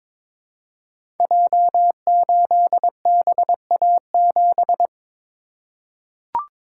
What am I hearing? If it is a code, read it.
Morse code, heard as a keyed tone, about 22 wpm, 700 Hz. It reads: J8BA7
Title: Code Groups - 5 Letter:Numbers - 22wpm